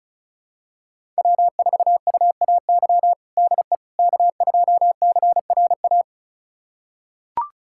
Morse code, heard as a keyed tone, about 36 wpm, 700 Hz. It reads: W4UAY DE K2CRA
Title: USA Callsign DE Callsign - 5 Characters - 35wpm